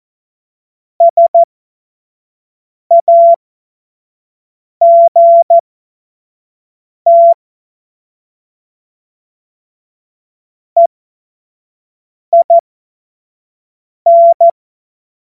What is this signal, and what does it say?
Morse code, heard as a keyed tone, about 14 wpm, spaced as for 5 wpm, 680 Hz. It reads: SAGT EIN
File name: Morsekurs / Witze / Witz 10.Oktober 2023 25 F70